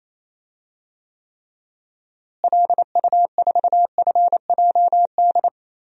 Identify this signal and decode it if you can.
Morse code, heard as a keyed tone, about 28 wpm, 700 Hz. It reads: LU4FJD
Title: DXCC-100 Call Signs - 6 Characters - Repeated 3x - 28wpm